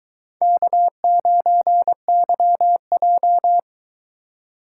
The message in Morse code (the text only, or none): K9YJ